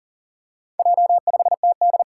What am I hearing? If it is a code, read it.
Morse code, heard as a keyed tone, about 44 wpm, 700 Hz. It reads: J5TD
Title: Four Challenging Letter-Number Combinations 40wpm